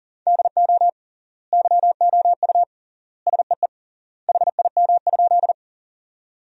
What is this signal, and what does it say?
Morse code, heard as a keyed tone, about 40 wpm, 700 Hz. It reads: DO YOU SEE HIM?
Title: 3x Word-Spacing - Sentences from Top 100 Words - 40wpm